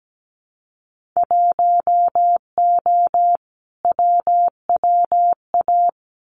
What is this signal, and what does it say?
Morse code, heard as a keyed tone, about 17 wpm, 700 Hz. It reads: 1O WWA
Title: Winter Field Day Exchanges 17wpm